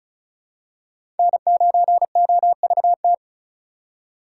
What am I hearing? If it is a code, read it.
Morse code, heard as a keyed tone, about 36 wpm, 700 Hz. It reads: N9OVT
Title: USA Call Signs - 5 Characters - 3x Character Spacing - 35wpm